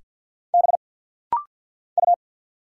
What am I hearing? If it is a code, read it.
Morse code, heard as a keyed tone, about 50 wpm, 700 Hz. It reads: B U
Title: Single Letters - Rapid-Fire 50wpm